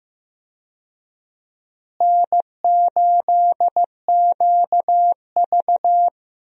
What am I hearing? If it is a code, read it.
Morse code, heard as a keyed tone, about 15 wpm, 700 Hz. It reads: N8QV
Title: USA Call Signs - 4 Characters - Repeated 2x - 15wpm